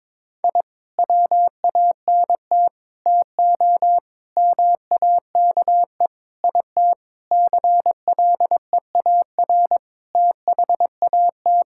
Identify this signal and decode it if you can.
Morse code, heard as a keyed tone, about 22 wpm, 700 Hz. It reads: I WANT TO MAKE IT CLEAR THAT
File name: Sentences with 7-Word Phrases - 22wpm